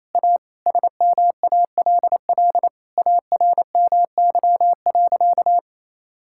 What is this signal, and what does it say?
Morse code, heard as a keyed tone, about 27 wpm, 700 Hz. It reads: A SMALL ARMY.
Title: Sentences from Top 1000 Words - Review Part 6 - 28wpm